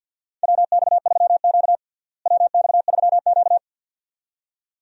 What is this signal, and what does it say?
Morse code, heard as a keyed tone, about 51 wpm, 700 Hz. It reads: WX3Q WX3Q
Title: USA Call Signs - 4 Characters - Repeated 2x - 50wpm